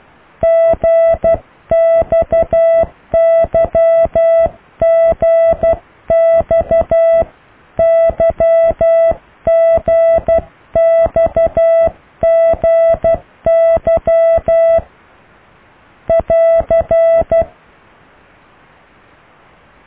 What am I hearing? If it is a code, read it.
Morse code, heard as a keyed tone, about 12 wpm, 660 Hz. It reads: GXYGXYGXGY <AR>